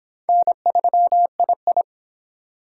Morse code, heard as a keyed tone, about 26 wpm, 700 Hz. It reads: N3II
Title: Speed Racing - USA 1x2 Call Signs 22wpm